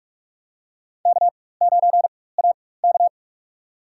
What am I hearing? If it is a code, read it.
Morse code, heard as a keyed tone, about 42 wpm, spaced as for 20 wpm, 700 Hz. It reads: K9AK
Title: USA Call Signs - 4 Characters - 4x Character Spacing - 45wpm